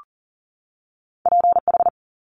Morse code, heard as a keyed tone, about 40 wpm, 700 Hz. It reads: PH
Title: Top Two Letter Combinations in Words 40wpm